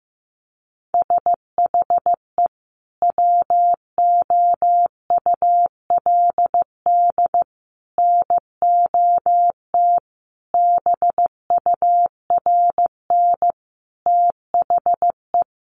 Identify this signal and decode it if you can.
Morse code, heard as a keyed tone, about 15 wpm, 700 Hz. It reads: SHE WOULD NOT BURN THE
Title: Sentences from Top 1300 Words 15wpm